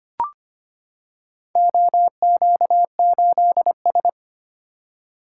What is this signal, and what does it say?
Morse code, heard as a keyed tone, about 26 wpm, 700 Hz. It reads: OQ8S
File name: practice / 25wpm / Four Letter-Number Combinations - 25wpm